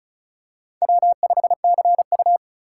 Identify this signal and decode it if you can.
Morse code, heard as a keyed tone, about 36 wpm, 700 Hz. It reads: W5CU
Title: USA Call Signs - 4 Characters - 2x Character Spacing - 35wpm